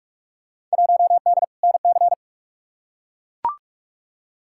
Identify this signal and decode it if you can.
Morse code, heard as a keyed tone, about 42 wpm, 700 Hz. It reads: 1D NC
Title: Field Day Exchanges Encore - 45wpm